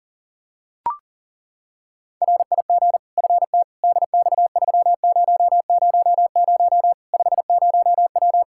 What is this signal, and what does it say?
Morse code, heard as a keyed tone, about 40 wpm, 700 Hz. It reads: RIG FT DX3000 50W
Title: Practice QSO - Exchange 2 - Info - 40wpm